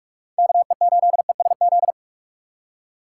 Morse code, heard as a keyed tone, about 45 wpm, 700 Hz. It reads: KE8ESZ
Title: USA Call Signs - 6 Characters - 6x Character Spacing - 45wpm